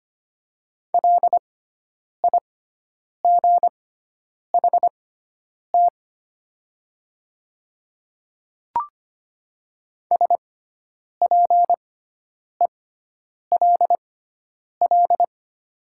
Morse code, heard as a keyed tone, about 25 wpm, spaced as for 9 wpm, 700 Hz. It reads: LIGHT SPELL
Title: 6x Character Spacing - Top 300 Words - 25wpm